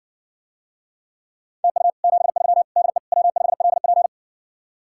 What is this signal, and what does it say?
Morse code, heard as a keyed tone, about 60 wpm, 700 Hz. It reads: TU 73 DE W5LP